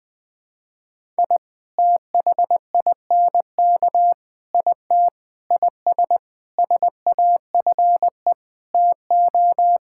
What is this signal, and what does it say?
Morse code, heard as a keyed tone, about 21 wpm, 700 Hz. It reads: I THINK IT IS SAFE TO
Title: Sets of 8 Words - 20wpm